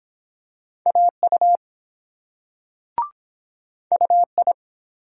Morse code, heard as a keyed tone, about 23 wpm, 700 Hz. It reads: AU UI